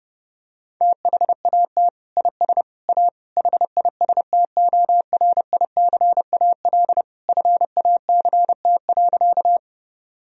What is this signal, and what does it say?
Morse code, heard as a keyed tone, about 30 wpm, 700 Hz. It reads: THAT IS A HISTORICAL FACT.
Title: Sentences from Top 1000 Words - Review Part 4 - 30wpm